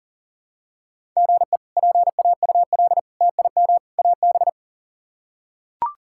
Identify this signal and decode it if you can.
Morse code, heard as a keyed tone, about 38 wpm, 700 Hz. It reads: GE PAUL TIM AB
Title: K1USN SST QSO Practice - 40wpm